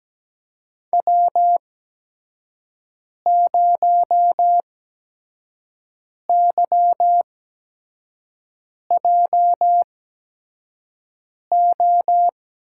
Morse code, heard as a keyed tone, about 17 wpm, spaced as for 5 wpm, 700 Hz. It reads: W0YJO
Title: USA Call Signs - 5 Characters - 8x Character Spacing - 17wpm